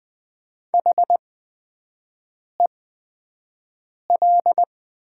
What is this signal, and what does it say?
Morse code, heard as a keyed tone, about 20 wpm, spaced as for 5 wpm, 700 Hz. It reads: HEL
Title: 8x Character Spacing - Top 200 Words - 20wpm